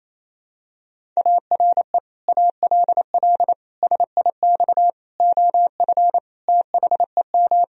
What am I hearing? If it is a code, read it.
Morse code, heard as a keyed tone, about 28 wpm, 690 Hz. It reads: ARE ALL SIX OF THEM
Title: Sentences from Top 1000 Words - Review Part 2 - 28wpm